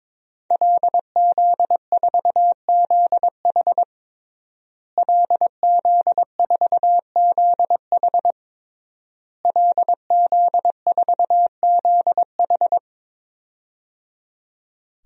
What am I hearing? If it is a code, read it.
Morse code, heard as a keyed tone, about 22 wpm, 700 Hz. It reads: LZ4ZH LZ4ZH LZ4ZH
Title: DXCC-100 Call Signs - 5 Characters - Repeated 3x - 22wpm